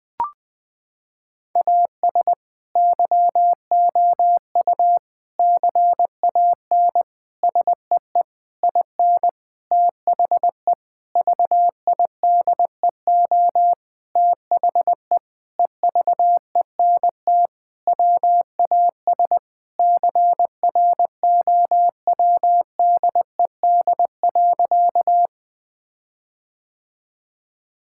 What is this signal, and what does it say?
Morse code, heard as a keyed tone, about 20 wpm, 700 Hz. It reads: AS YOU CAN SEE IN THE VIDEO THE EVENT WAS CROWDED.
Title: Sentences with 7-Word Phrases - 20wpm